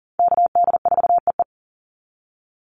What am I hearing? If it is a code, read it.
Morse code, heard as a keyed tone, about 40 wpm, 700 Hz. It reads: KD4EE
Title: Call Signs - 40wpm